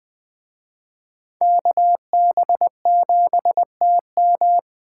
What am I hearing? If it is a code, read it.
Morse code, heard as a keyed tone, about 21 wpm, 700 Hz. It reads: KB7TM